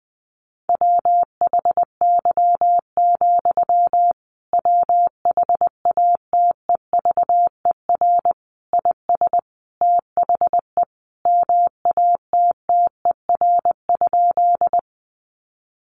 Morse code, heard as a keyed tone, about 20 wpm, 700 Hz. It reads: WHY, WHATEVER IS THE MATTER?
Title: Sentences from Top 1100 Words 20wpm